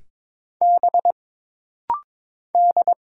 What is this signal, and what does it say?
Morse code, heard as a keyed tone, about 22 wpm, 700 Hz. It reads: B D
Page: This rapid-fire timing is geared to push your ICR (Instant Character Recognition) skills.